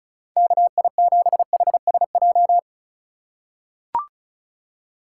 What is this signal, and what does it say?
Morse code, heard as a keyed tone, about 34 wpm, 700 Hz. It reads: KI7HSJ
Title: USA Call Signs - 6 Characters - 4x Character Spacing - 35wpm